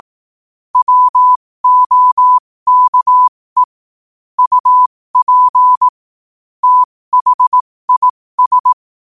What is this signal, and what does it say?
Morse code, heard as a keyed tone, about 18 wpm, spaced as for 15 wpm, 1000 Hz. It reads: WOKE UP THIS
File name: qotdfiles / 15wpm